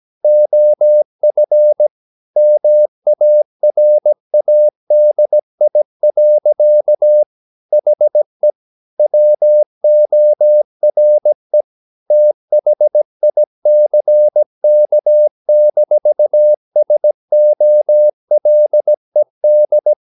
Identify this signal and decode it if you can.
Morse code, heard as a keyed tone, about 17 wpm, 600 Hz. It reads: OF MARADI. HE WORE THICK-SOLED
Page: Romanen Never af Ken Follet på engelsk.